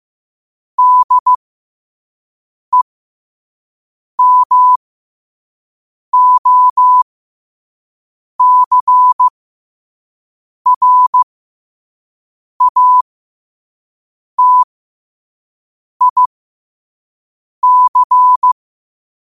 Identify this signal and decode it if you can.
Morse code, heard as a keyed tone, about 15 wpm, spaced as for 5 wpm, 1000 Hz. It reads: DEMOCRATIC
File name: qotdfiles / 5wpm